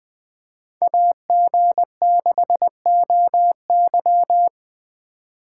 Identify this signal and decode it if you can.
Morse code, heard as a keyed tone, about 20 wpm, 700 Hz. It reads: AG6OY